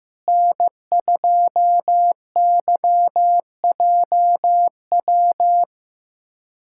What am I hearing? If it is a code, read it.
Morse code, heard as a keyed tone, about 15 wpm, 700 Hz. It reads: N2YJW
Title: USA Callsign DE Callsign 15wpm